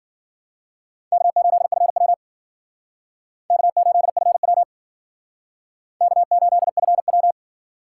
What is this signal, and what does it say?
Morse code, heard as a keyed tone, about 56 wpm, 700 Hz. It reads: K8FW K8FW K8FW